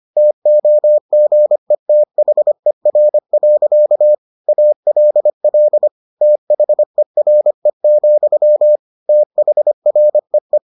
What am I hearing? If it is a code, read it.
Morse code, heard as a keyed tone, about 25 wpm, 600 Hz. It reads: TOGETHER. ALL THERE, THREE